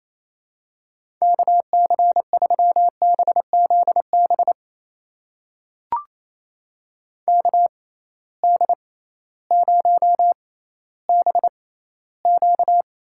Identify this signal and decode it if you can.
Morse code, heard as a keyed tone, about 28 wpm, spaced as for 15 wpm, 700 Hz. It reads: KC3BZB KD0BQ
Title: USA Call Signs - 6 Characters - 6x Character Spacing - 28wpm